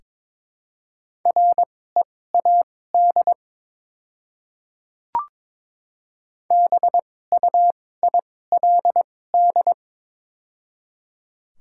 Morse code, heard as a keyed tone, about 23 wpm, spaced as for 16 wpm, 700 Hz. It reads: READ BUILD
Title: 2x Character Spacing - Top 500 Words - 22wpm